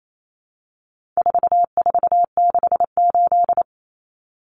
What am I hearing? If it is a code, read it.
Morse code, heard as a keyed tone, about 28 wpm, 700 Hz. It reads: 4468
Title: Numbers - 4-Digit 28wpm